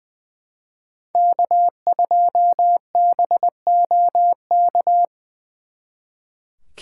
Morse code, heard as a keyed tone, about 25 wpm, 700 Hz. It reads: K2BOK
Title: Speed Racing - USA 1x3 Call Signs 20wpm